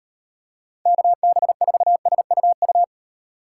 Morse code, heard as a keyed tone, about 38 wpm, 700 Hz. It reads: KB4SUU